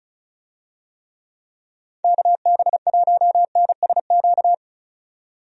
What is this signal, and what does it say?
Morse code, heard as a keyed tone, about 36 wpm, 700 Hz. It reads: KB1NSQ
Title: USA Call Signs - 6 Characters - 35wpm